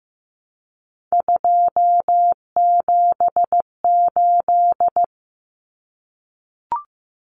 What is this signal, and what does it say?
Morse code, heard as a keyed tone, about 15 wpm, 700 Hz. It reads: 278
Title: Common Cut Numbers - 3-Digit 15wpm